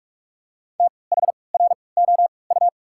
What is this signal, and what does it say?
Morse code, heard as a keyed tone, about 46 wpm, spaced as for 26 wpm, 700 Hz. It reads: THROU
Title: 3x Character Spacing - Top 300 Words - 45wpm